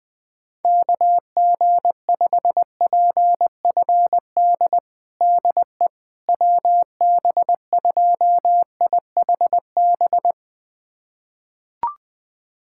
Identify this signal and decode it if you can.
Morse code, heard as a keyed tone, about 20 wpm, 700 Hz. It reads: KG5PFD DE WB2IHB
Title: USA Callsign DE Callsign 20wpm